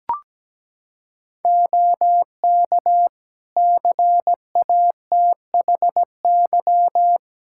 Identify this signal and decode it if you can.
Morse code, heard as a keyed tone, about 17 wpm, 700 Hz. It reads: OK CATHY
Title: K1USN SST QSO Practice 17wpm